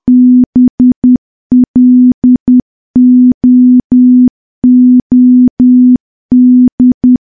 Morse code, heard as a keyed tone, about 10 wpm, 250 Hz. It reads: BLOOD
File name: morse / morse-training-10WPM